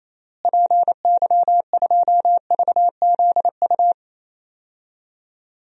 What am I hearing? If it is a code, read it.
Morse code, heard as a keyed tone, about 27 wpm, 700 Hz. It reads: PY2VZU
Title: DXCC-100 Call Signs - 6 Characters - Repeated 3x - 28wpm